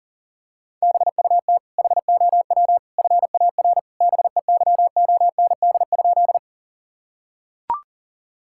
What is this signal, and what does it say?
Morse code, heard as a keyed tone, about 40 wpm, 700 Hz. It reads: BUT HOW FAR BEYOND?
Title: Sentences from Top 1000 Words - Review Part 6 - 40wpm